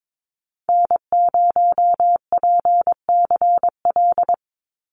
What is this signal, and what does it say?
Morse code, heard as a keyed tone, about 22 wpm, 700 Hz. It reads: N0PCL
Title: NA SOTA Activator Callsigns 22wpm